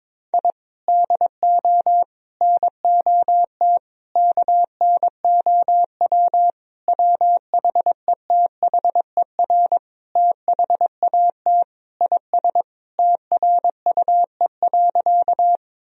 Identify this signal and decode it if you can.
Morse code, heard as a keyed tone, about 22 wpm, 700 Hz. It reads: I DO NOT KNOW WHETHER THAT IS TRUE.
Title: Sentences from Top 1000 Words - Review Part 3 - 22wpm